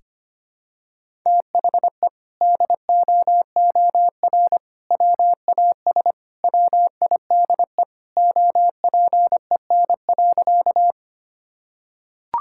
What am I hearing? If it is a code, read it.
Morse code, heard as a keyed tone, about 25 wpm, 700 Hz. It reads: THE DOOR WAS WIDE OPEN.
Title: Sentences from Top 1000 Words - Review Part 6 - 25wpm